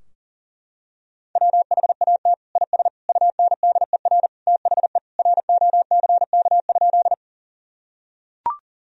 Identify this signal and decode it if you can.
Morse code, heard as a keyed tone, about 37 wpm, 700 Hz. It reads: WHAT IS UNDER THE ROCK?
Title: Sentences from Top 1000 Words - Review Part 2 - 40wpm